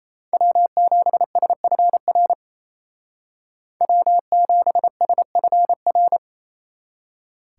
Morse code, heard as a keyed tone, about 36 wpm, 700 Hz. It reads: W7SFR W7SFR